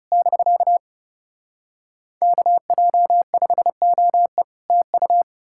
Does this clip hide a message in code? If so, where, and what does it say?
Yes: Morse code from 0.1 to 5.2 s, heard as a keyed tone, about 40 wpm, 700 Hz. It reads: <BK> KJ5OE TU